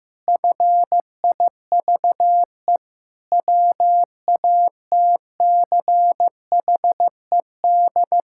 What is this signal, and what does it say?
Morse code, heard as a keyed tone, about 15 wpm, 700 Hz. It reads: FIVE WATCHED
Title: Pangram Challenge v2 - 15wpm